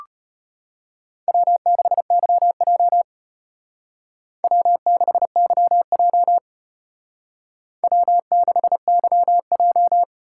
Morse code, heard as a keyed tone, about 32 wpm, 700 Hz. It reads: W6YJ W6YJ W6YJ